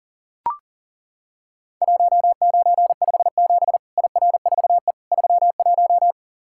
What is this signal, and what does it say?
Morse code, heard as a keyed tone, about 40 wpm, 700 Hz. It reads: 1957 IR4E 31
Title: Worked All Europe DX Contest - QTC Exchange - 40wpm